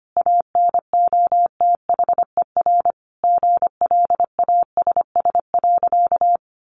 Morse code, heard as a keyed tone, about 25 wpm, 700 Hz. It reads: ANOTHER GLASS.
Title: Sentences from Top 1500 Words 25wpm